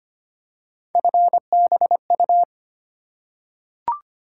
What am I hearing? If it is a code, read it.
Morse code, heard as a keyed tone, about 25 wpm, 700 Hz. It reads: FBU
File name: Three Letter-Number Combinations - 25wpm